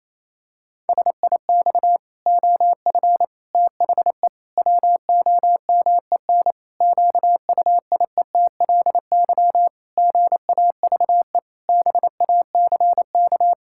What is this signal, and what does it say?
Morse code, heard as a keyed tone, about 28 wpm, 700 Hz. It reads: SIX OF THE WOMEN QUIETLY GAVE BACK
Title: Pangram Challenge v2 - 28wpm